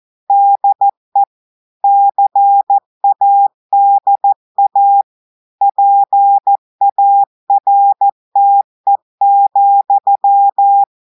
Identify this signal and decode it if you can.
Morse code, heard as a keyed tone, about 14 wpm, 800 Hz. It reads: DE CADA PARTE,
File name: audios / portugais / portugais texte en clair 1 vitesse 14 mots minutes